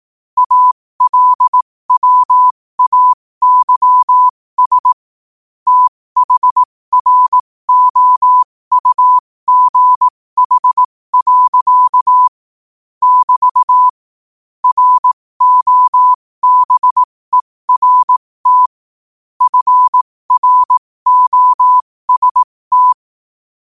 Morse code, heard as a keyed tone, about 18 wpm, spaced as for 15 wpm, 1000 Hz. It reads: ALWAYS THROUGH. <BT> ROBERT FROST
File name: qotdfiles / 15wpm